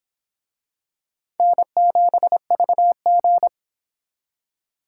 Morse code, heard as a keyed tone, about 25 wpm, 700 Hz. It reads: N7VG